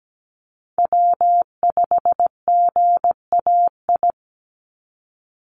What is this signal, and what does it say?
Morse code, heard as a keyed tone, about 17 wpm, 700 Hz. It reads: W5GAI
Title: Call Signs Repeated 2x - 17wpm